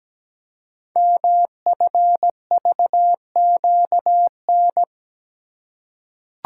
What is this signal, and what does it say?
Morse code, heard as a keyed tone, about 17 wpm, 700 Hz. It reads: MFVQN
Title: Code Groups - 5 Letters - 17wpm